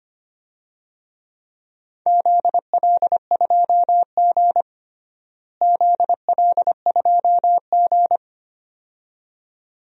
Morse code, heard as a keyed tone, about 25 wpm, 700 Hz. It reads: ZL2G ZL2G
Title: DXCC-100 Call Signs - 4 Characters - Repeated 2x - 25wpm